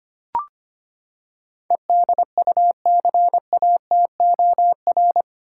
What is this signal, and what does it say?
Morse code, heard as a keyed tone, about 25 wpm, 700 Hz. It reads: EDUCATOR
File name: Eight Letter Words - 25wpm